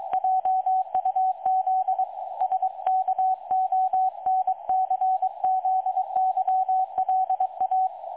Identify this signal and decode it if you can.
Morse code, heard as a keyed tone, about 22 wpm, 750 Hz. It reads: JUZ SKONCZYLA
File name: dlazaawansowanych / nagraniazpasma / websdr recording 3565.0kHz